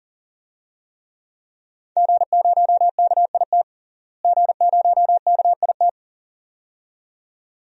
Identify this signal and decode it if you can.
Morse code, heard as a keyed tone, about 40 wpm, 700 Hz. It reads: G0KIT G0KIT
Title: DXCC-100 Call Signs - 5 Characters - Repeated 2x - 40wpm